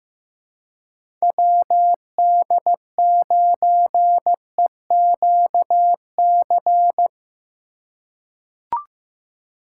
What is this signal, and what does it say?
Morse code, heard as a keyed tone, about 15 wpm, 700 Hz. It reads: WD9EQC
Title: USA Call Signs - 6 Characters - 4x Character Spacing - 15wpm